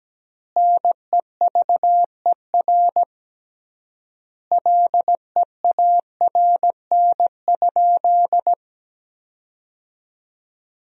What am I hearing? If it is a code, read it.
Morse code, heard as a keyed tone, about 17 wpm, 700 Hz. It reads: NEVER LEARN?
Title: 3x Word-Spacing - Sentences from Top 300 Words - 17wpm